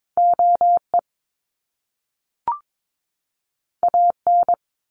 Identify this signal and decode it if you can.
Morse code, heard as a keyed tone, about 22 wpm, 700 Hz. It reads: OE AN
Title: Top Two Letter Combinations in Words 22wpm